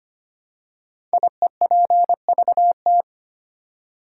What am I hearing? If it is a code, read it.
Morse code, heard as a keyed tone, about 25 wpm, 700 Hz. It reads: IEPVT
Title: Code Groups - 5 Letters - 25wpm